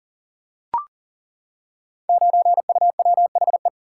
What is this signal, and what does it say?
Morse code, heard as a keyed tone, about 42 wpm, 700 Hz. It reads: 9UWHE
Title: Code Groups - 5 Letter:Numbers - 40wpm